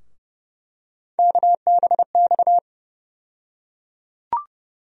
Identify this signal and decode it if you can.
Morse code, heard as a keyed tone, about 32 wpm, 700 Hz. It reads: KBX
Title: USA Call Sign Postfixes - 3 Letters 30wpm